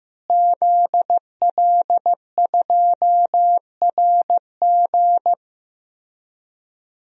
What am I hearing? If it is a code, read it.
Morse code, heard as a keyed tone, about 16 wpm, 700 Hz. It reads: ZL2RG
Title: DXCC-100 Call Signs - 5 Characters - Repeated 3x - 15wpm